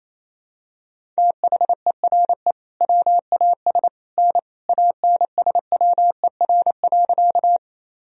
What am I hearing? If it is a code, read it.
Morse code, heard as a keyed tone, about 28 wpm, 700 Hz. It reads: THERE WAS N ANSWER.
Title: Sentences from Top 300 Words and Abbreviations 28wpm